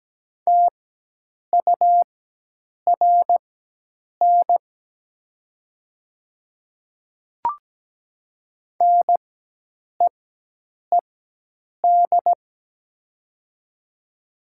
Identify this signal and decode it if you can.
Morse code, heard as a keyed tone, about 17 wpm, spaced as for 8 wpm, 700 Hz. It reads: TURN NEED
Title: 4x Character Spacing - Top 200 Words - 17wpm